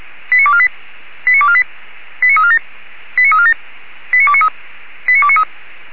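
Five-tone selective call (ZVEI), heard as quick runs of five tones, tones of about 70 ms. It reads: 89248 89248 89347 89347 89293 89293
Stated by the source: ZVEI five-tone selective call received on an SDR.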